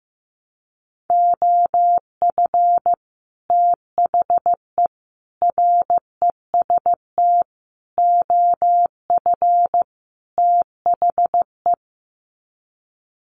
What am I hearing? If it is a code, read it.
Morse code, heard as a keyed tone, about 15 wpm, 700 Hz. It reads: OF THE REST OF THE
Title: Sets of 5 Words 15wpm